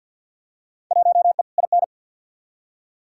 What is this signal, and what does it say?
Morse code, heard as a keyed tone, about 52 wpm, 700 Hz. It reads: 1E IN
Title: Field Day Exchanges - 50wpm